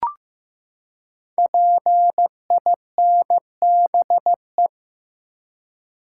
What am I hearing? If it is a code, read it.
Morse code, heard as a keyed tone, about 15 wpm, 700 Hz. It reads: PINBE
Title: Code Groups - 5 Letter:Numbers - 15wpm